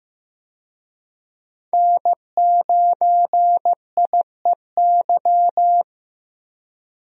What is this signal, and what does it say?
Morse code, heard as a keyed tone, about 15 wpm, 700 Hz. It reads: N9IEY
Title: USA Call Signs - 5 Characters - Repeated 2x - 15wpm